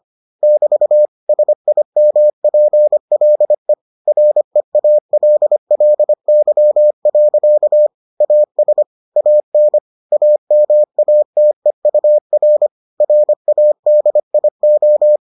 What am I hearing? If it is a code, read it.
Morse code, heard as a keyed tone, about 25 wpm, 600 Hz. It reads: <BT> SIMPLE REALLY. AS AN AMATEUR RADIO